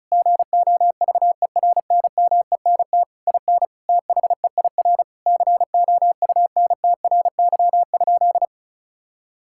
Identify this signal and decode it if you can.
Morse code, heard as a keyed tone, about 34 wpm, 700 Hz. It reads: GOVERNMENT IN THEIR COUNTRY?
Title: Sentences from Top 1000 Words - Review Part 5 - 35wpm